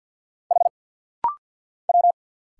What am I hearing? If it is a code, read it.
Morse code, heard as a keyed tone, about 49 wpm, 690 Hz. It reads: H W
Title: Single Letter-Number Alt - Rapid-Fire 50wpm